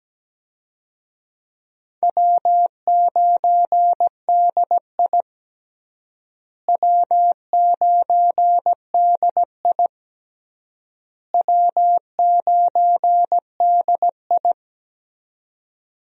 Morse code, heard as a keyed tone, about 17 wpm, 700 Hz. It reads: W9DI W9DI W9DI
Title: USA Call Signs - 4 Characters - Repeated 3x - 17wpm